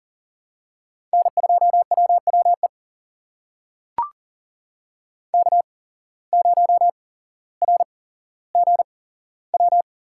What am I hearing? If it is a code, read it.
Morse code, heard as a keyed tone, about 40 wpm, spaced as for 17 wpm, 700 Hz. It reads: N2WWE K0RGW
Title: USA Call Signs - 5 Characters - 8x Character Spacing - 40wpm